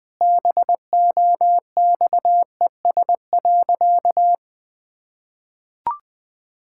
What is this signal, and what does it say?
Morse code, heard as a keyed tone, about 20 wpm, 700 Hz. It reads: BOXES.
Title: Pangram Challenge v2 - 20wpm